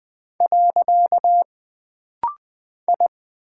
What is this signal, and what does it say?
Morse code, heard as a keyed tone, about 20 wpm, 700 Hz. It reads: . I
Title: Single Character (Letter:Number:Punctuation:Prosign) - Rapid-Fire - 20wpm